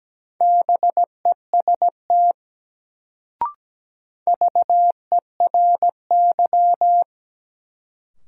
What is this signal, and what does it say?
Morse code, heard as a keyed tone, about 17 wpm, 700 Hz. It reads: BEST VERY
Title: Sets of 2 Words - New Word Drill - 17wpm